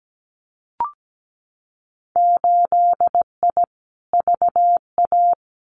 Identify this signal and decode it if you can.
Morse code, heard as a keyed tone, about 17 wpm, 700 Hz. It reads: 8I VA
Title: Winter Field Day Exchanges Encore 17wpm